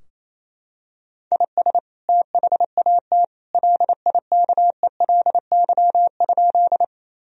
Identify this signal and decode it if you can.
Morse code, heard as a keyed tone, about 28 wpm, 700 Hz. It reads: IS THAT LIKELY?
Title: Sentences from Top 1000 Words - Review Part 4 - 28wpm